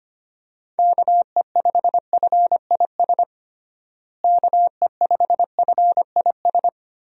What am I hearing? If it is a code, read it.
Morse code, heard as a keyed tone, about 25 wpm, 700 Hz. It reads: KE5FIS KE5FIS
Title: USA Call Signs - 6 Characters - Repeated 2x - 25wpm